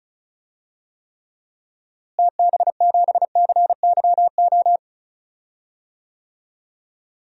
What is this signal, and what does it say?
Morse code, heard as a keyed tone, about 34 wpm, 700 Hz. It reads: TB7CYO
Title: DXCC-100 Call Signs - 6 Characters - Repeated 3x - 35wpm